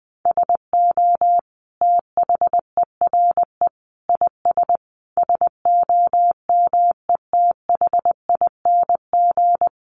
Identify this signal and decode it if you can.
Morse code, heard as a keyed tone, about 19 wpm, 700 Hz. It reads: SO THERE IS SOMETHING
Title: Sentences from Top 1500 Words 20wpm